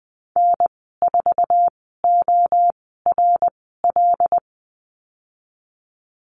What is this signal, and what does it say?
Morse code, heard as a keyed tone, about 19 wpm, spaced as for 14 wpm, 700 Hz. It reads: N4ORL
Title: 154 Lesson 25g - Letter p - Call Signs 20wpm